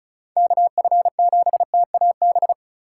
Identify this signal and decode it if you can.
Morse code, heard as a keyed tone, about 36 wpm, 700 Hz. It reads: KF7TAB
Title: USA Call Signs - 4x Character Spacing - 35wpm